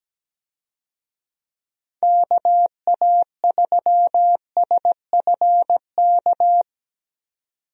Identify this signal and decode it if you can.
Morse code, heard as a keyed tone, about 17 wpm, 700 Hz. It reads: KA3SFK
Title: USA Call Signs - 6 Characters - 17wpm